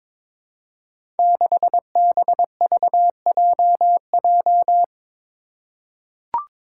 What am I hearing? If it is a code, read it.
Morse code, heard as a keyed tone, about 22 wpm, 700 Hz. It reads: 6BVJJ
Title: Five Challenging Letter-Number Combinations 22wpm